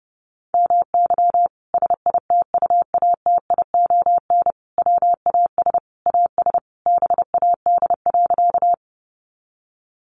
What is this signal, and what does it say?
Morse code, heard as a keyed tone, about 28 wpm, 700 Hz. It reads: MY SITUATION WAS AS BAD.
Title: Sentences from Top 1200 Words 30wpm